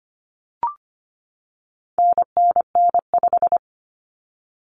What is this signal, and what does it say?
Morse code, heard as a keyed tone, about 25 wpm, 700 Hz. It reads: NNN5
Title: Common Cut Numbers - 4-Digit 25wpm